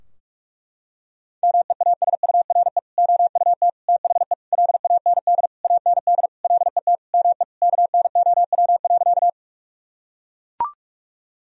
Morse code, heard as a keyed tone, about 41 wpm, 700 Hz. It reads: MEASURE OUT THE LAND AND LET ME KNOW.
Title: Sentences from Top 1000 Words - Review Part 2 - 45wpm